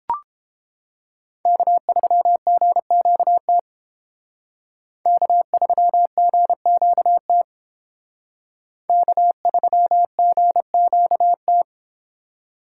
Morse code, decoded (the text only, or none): K3GQT K3GQT K3GQT